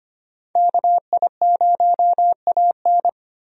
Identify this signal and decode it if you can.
Morse code, heard as a keyed tone, about 25 wpm, 700 Hz. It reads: KI0AN
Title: USA Call Signs - 5 Characters - 25wpm